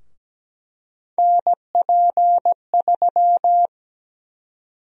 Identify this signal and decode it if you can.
Morse code, heard as a keyed tone, about 17 wpm, 700 Hz. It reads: NP3
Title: USA Call Sign Prefixes - Repeated 2x - 17wpm